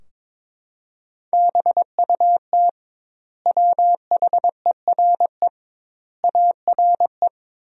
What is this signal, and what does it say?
Morse code, heard as a keyed tone, about 21 wpm, 700 Hz. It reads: BUT WHERE ARE
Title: 2x Word-Spacing - Sentences from Top 200 Words - 22wpm